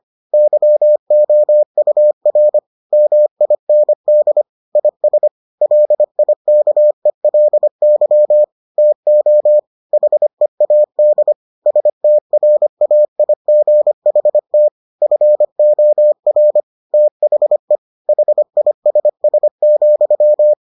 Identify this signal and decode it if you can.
Morse code, heard as a keyed tone, about 25 wpm, 600 Hz. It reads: YOUR MIND IS LIKELY TO HEAD STRAIGHT FOR THE HISS,